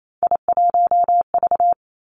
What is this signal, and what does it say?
Morse code, heard as a keyed tone, about 28 wpm, 700 Hz. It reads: I1V
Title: Three Challenging Letter-Number Combinations 28wpm